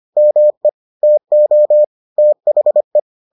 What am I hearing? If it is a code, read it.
Morse code, heard as a keyed tone, about 25 wpm, 600 Hz. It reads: ME TO THE